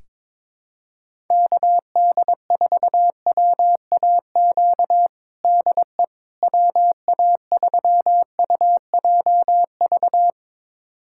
Morse code, heard as a keyed tone, about 22 wpm, 700 Hz. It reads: KD4WAQ DE WA3UJV
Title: USA Callsign DE Callsign 22wpm